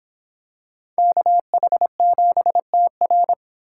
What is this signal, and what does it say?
Morse code, heard as a keyed tone, about 25 wpm, 700 Hz. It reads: KH7TR